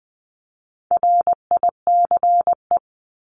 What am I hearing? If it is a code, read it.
Morse code, heard as a keyed tone, about 20 wpm, 700 Hz. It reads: RICE
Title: Top 1501-1750 Words 20wpm